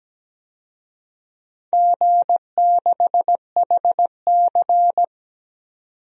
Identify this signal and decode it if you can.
Morse code, heard as a keyed tone, about 17 wpm, 700 Hz. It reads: G6HC
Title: DXCC-100 Call Signs - 4 Characters - 17wpm